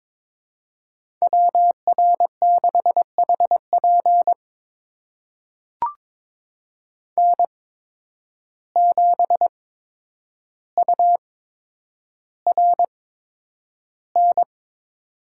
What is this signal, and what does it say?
Morse code, heard as a keyed tone, about 22 wpm, spaced as for 9 wpm, 700 Hz. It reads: WR6HP N7URN
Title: USA Call Signs - 5 Characters - 8x Character Spacing - 22wpm